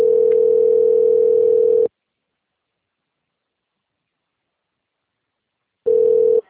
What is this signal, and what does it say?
Ringback tone (North America), 440 + 480 Hz.